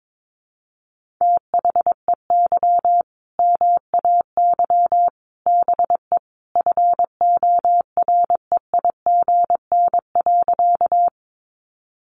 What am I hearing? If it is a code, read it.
Morse code, heard as a keyed tone, about 22 wpm, 700 Hz. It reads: THEY MAY BE FOREIGN.
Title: Sentences from Top 1400 Words 22wpm